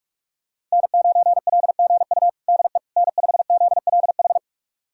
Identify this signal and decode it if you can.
Morse code, heard as a keyed tone, about 46 wpm, 700 Hz. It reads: N9LGU DE N5ZLH
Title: USA Callsign DE Callsign - 5 Characters - 45wpm